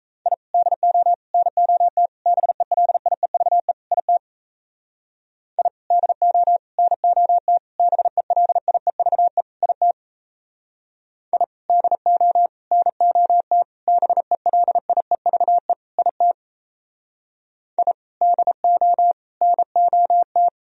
Morse code, heard as a keyed tone, about 34 wpm, 700 Hz. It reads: I DO NOT BELIEVE IT I DO NOT BELIEVE IT I DO NOT BELIEVE IT I DO NOT
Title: Speed-Racing - Sets of 5 Words 28wpm